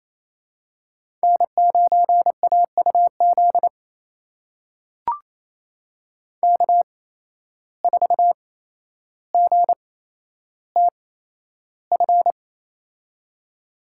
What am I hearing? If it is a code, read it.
Morse code, heard as a keyed tone, about 28 wpm, spaced as for 12 wpm, 700 Hz. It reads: N9AUZ K4GTF
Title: USA Call Signs - 5 Characters - 8x Character Spacing - 28wpm